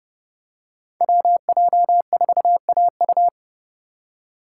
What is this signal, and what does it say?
Morse code, heard as a keyed tone, about 30 wpm, 700 Hz. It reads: WJ4AU
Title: Code Groups - 5 Letter:Numbers - 30wpm